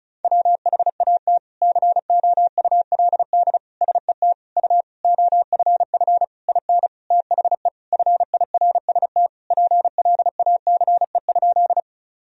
Morse code, heard as a keyed tone, about 34 wpm, 700 Hz. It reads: WHAT COULD SET U OFF IN THE FIRST PLACE?
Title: Sentences from Top 200 Words and Abbreviations 35wpm